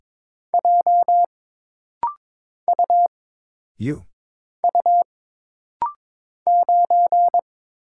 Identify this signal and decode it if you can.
Morse code, heard as a keyed tone, about 22 wpm, 700 Hz. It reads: J U U 9
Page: This rapid-fire timing is geared to push your ICR (Instant Character Recognition) skills.